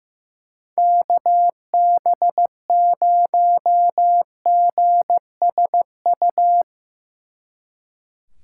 Morse code, heard as a keyed tone, about 15 wpm, 700 Hz. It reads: KB0GSU
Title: Speed Racing - USA 2x3 Call Signs 15wpm